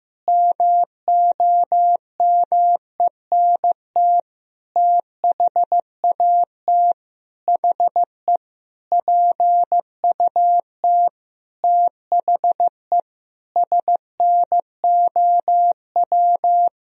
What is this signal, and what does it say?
Morse code, heard as a keyed tone, about 16 wpm, 700 Hz. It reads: MOMENT THAT HE PUT THE SNOW
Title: Sentences from Top 1000 Words - Review Part 3 - 15wpm